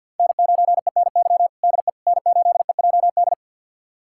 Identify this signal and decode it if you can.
Morse code, heard as a keyed tone, about 50 wpm, 700 Hz. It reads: N9ENY DE N8EJD
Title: USA Callsign DE Callsign - 5 Characters - 50wpm